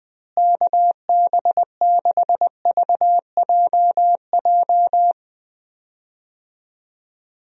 Morse code, heard as a keyed tone, about 20 wpm, 700 Hz. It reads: KB6VJJ
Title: USA Call Signs - 6 Characters - Repeated 3x - 20wpm